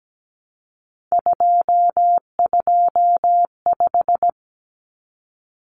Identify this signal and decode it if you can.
Morse code, heard as a keyed tone, about 17 wpm, 700 Hz. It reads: 225
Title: Common Cut Numbers - 3-Digit 17wpm